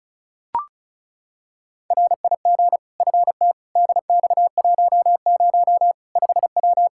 Morse code, heard as a keyed tone, about 36 wpm, 700 Hz. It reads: RIG FT DX10 5W
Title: Practice QSO - Exchange 2 - Info - 35wpm